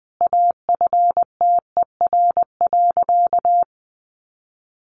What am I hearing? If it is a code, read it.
Morse code, heard as a keyed tone, about 20 wpm, 700 Hz. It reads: AFTER.
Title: Sentences from Top 1500 Words 20wpm